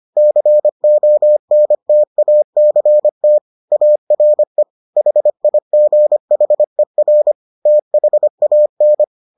Morse code, heard as a keyed tone, about 25 wpm, 600 Hz. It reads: CONTACT ARE HIGHER THAN